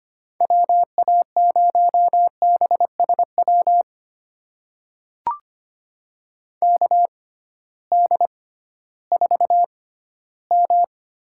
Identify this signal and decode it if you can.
Morse code, heard as a keyed tone, about 25 wpm, spaced as for 15 wpm, 700 Hz. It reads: WA0BSW KD4M
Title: USA Call Signs - 6 Characters - 6x Character Spacing - 25wpm